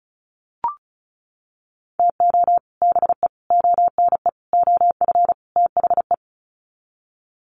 Morse code, heard as a keyed tone, about 34 wpm, 700 Hz. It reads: TO BE ONE OF THE
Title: Sets of 5 Words 35wpm